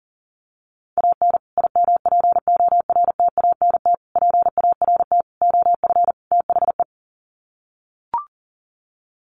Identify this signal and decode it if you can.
Morse code, heard as a keyed tone, about 40 wpm, 700 Hz. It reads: AN IMPORTANT PART OF THE
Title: Sets of 5 Words Encore 40wpm